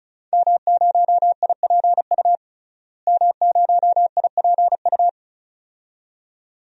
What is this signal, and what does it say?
Morse code, heard as a keyed tone, about 36 wpm, 700 Hz. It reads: M0IPU M0IPU
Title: DXCC-100 Call Signs - 5 Characters - Repeated 2x - 35wpm